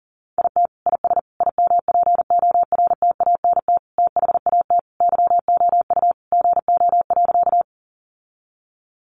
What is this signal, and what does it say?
Morse code, heard as a keyed tone, about 40 wpm, 700 Hz. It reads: IT IS IMPORTANT THAT YOU GO.
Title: Sentences from Top 600 Words 40wpm